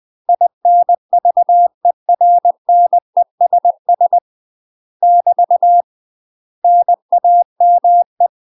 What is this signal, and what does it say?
Morse code, heard as a keyed tone, about 20 wpm, 700 Hz. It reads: INVERNESS <BT> NAME
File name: UK MASTER QSOs TRAINING / 20WPM / UK MASTER QSOs 0064